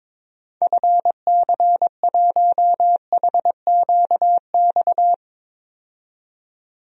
Morse code, heard as a keyed tone, about 21 wpm, 700 Hz. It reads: FC1HQX
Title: DXCC-100 Call Signs - 6 Characters - 22wpm